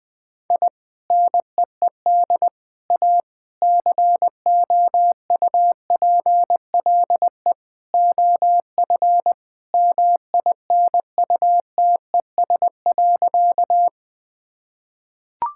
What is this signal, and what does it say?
Morse code, heard as a keyed tone, about 20 wpm, 700 Hz. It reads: I NEED A COUPLE OF MINUTES.
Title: Sentences with 3-Word Phrases - Spoken Beginning - 20wpm